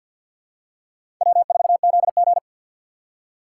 Morse code, heard as a keyed tone, about 50 wpm, 700 Hz. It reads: W4ZG
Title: USA Call Signs - 4 Characters - 6x Character Spacing - 50wpm